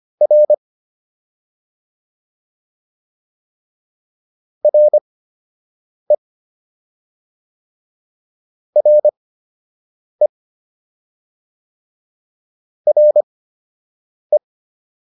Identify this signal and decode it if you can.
Morse code, heard as a keyed tone, about 25 wpm, spaced as for 7 wpm, 600 Hz. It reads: R RE RE RE